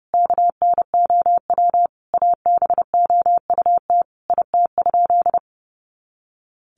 Morse code, heard as a keyed tone, about 30 wpm, 700 Hz. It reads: KNOW ABOUT IT?
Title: Sentences from Top 100 Words 30wpm